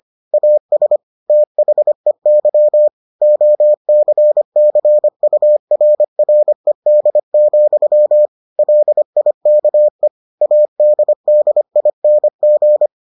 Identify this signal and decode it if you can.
Morse code, heard as a keyed tone, about 25 wpm, 600 Hz. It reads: AS THEY OCCURRED, LIKE ADDING